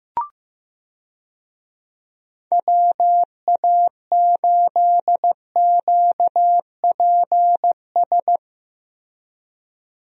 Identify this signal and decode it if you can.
Morse code, heard as a keyed tone, about 15 wpm, 700 Hz. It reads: WA8QPS
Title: USA Call Signs - 6 Characters - Repeated 3x - 15wpm